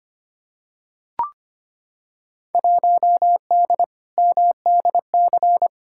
Morse code, heard as a keyed tone, about 25 wpm, 700 Hz. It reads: 1D MDC
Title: Field Day Exchanges - 25wpm